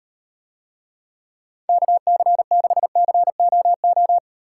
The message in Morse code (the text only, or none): KC6COO